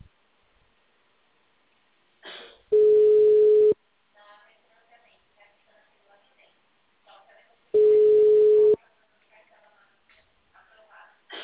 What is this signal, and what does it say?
Ringback tone (Europe, 425 Hz), with speech.